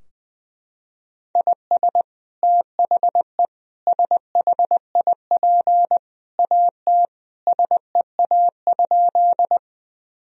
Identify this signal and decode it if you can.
Morse code, heard as a keyed tone, about 21 wpm, 700 Hz. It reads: IS THE SHIP AT SEA?
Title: Sentences from Top 1000 Words - Review Part 2 - 20wpm